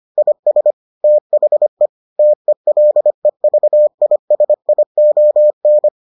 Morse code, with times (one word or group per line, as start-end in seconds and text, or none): 0.2-0.7 IS
1.0-1.9 THE
2.2-5.9 TELEVISION